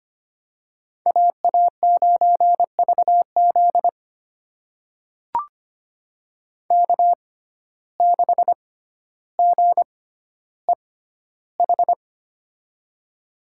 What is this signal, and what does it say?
Morse code, heard as a keyed tone, about 25 wpm, spaced as for 13 wpm, 700 Hz. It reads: AA9VZ K6GEH
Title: USA Call Signs - 5 Characters - 6x Character Spacing - 25wpm